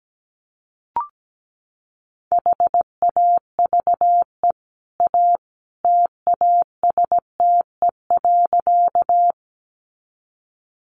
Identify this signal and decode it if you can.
Morse code, heard as a keyed tone, about 17 wpm, 700 Hz. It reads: HAVE A TASTE.
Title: Sentences from Top 1100 Words 17wpm